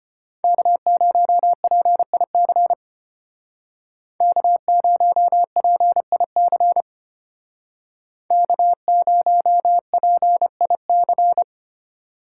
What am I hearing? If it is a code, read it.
Morse code, heard as a keyed tone, about 31 wpm, 700 Hz. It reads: K0PIC K0PIC K0PIC
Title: Speed Racing - USA Call Signs 25wpm